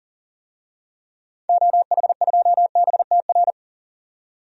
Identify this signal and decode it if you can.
Morse code, heard as a keyed tone, about 40 wpm, 700 Hz. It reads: OH2BTR
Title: DXCC-100 Call Signs - 6 Characters - 40wpm